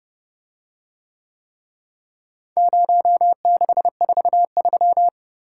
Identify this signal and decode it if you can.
Morse code, heard as a keyed tone, about 28 wpm, 700 Hz. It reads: 0643